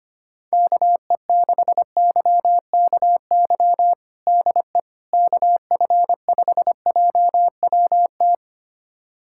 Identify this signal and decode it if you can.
Morse code, heard as a keyed tone, about 25 wpm, 700 Hz. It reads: KE6YKY DE KF5JWT
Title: USA Callsign DE Callsign - 6 Characters - 25wpm